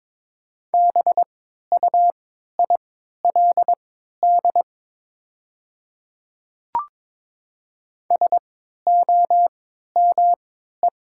Morse code, heard as a keyed tone, about 22 wpm, spaced as for 12 wpm, 700 Hz. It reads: BUILD SOME
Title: 3x Character Spacing - Top 200 Words - 22wpm